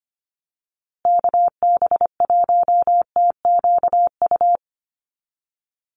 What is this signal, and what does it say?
Morse code, heard as a keyed tone, about 25 wpm, 700 Hz. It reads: KB1TQU
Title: USA Call Signs Repeated 2x 25wpm